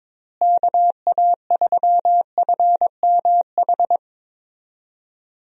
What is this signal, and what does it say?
Morse code, heard as a keyed tone, about 22 wpm, 700 Hz. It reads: KA3FMH
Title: USA Call Signs - 6 Characters - Repeated 2x - 22wpm